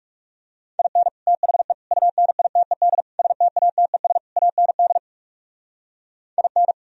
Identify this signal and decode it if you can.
Morse code, heard as a keyed tone, about 34 wpm, 700 Hz. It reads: IN THE UNITED STATES AND IN
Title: Speed-Racing - Sets of 5 Words 30wpm